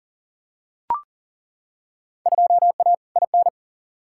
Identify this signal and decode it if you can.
Morse code, heard as a keyed tone, about 40 wpm, 700 Hz. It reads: 2A IN
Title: Field Day Exchanges - 40wpm